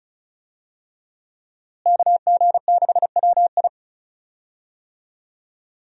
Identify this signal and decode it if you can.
Morse code, heard as a keyed tone, about 34 wpm, 700 Hz. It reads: KG6WI
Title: USA Call Signs - 5 Characters - Repeated 3x - 35wpm